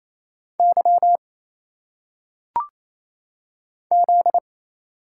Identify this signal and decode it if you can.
Morse code, heard as a keyed tone, about 27 wpm, 700 Hz. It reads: Y Z
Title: Single Letters - Phonetic Pronunciation 28wpm